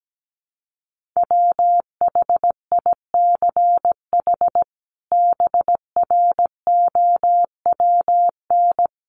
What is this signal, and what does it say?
Morse code, heard as a keyed tone, about 17 wpm, 700 Hz. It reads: WHICH BROWN
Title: Sentences from Top 1200 Words 17wpm